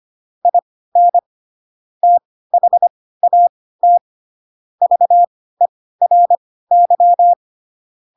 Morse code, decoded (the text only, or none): IN THAT VERY